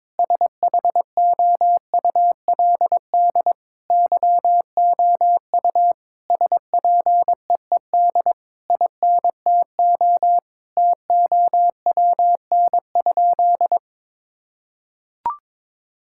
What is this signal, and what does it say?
Morse code, heard as a keyed tone, about 22 wpm, 700 Hz. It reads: SHOULD YOU SPEED INTO TOWN?
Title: Sentences from Top 1000 Words - Review Part 5 - 22wpm